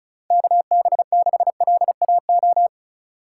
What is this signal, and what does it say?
Morse code, heard as a keyed tone, about 34 wpm, 700 Hz. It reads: KB6LAO
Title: USA Call Signs - 6 Characters - 2x Character Spacing - 35wpm